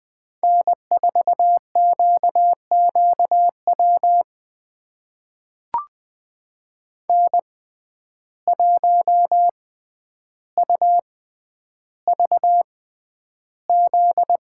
Morse code, heard as a keyed tone, about 20 wpm, spaced as for 10 wpm, 700 Hz. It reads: N4QQW N1UVZ
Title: USA Call Signs - 5 Characters - 6x Character Spacing - 20wpm